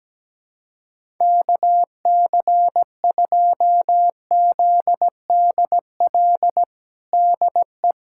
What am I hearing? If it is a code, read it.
Morse code, heard as a keyed tone, about 17 wpm, 700 Hz. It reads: KC2ZDL DE
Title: USA Callsign DE Callsign 17wpm